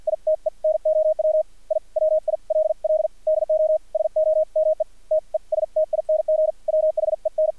Morse code, heard as a keyed tone, about 56 wpm, spaced as for 45 wpm, 620 Hz, with radio noise. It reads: ITE NOW I WILL DO SOME TESTING WHEN
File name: archived / CRSnet